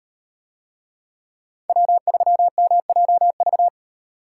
Morse code, heard as a keyed tone, about 38 wpm, 700 Hz. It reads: W3MJV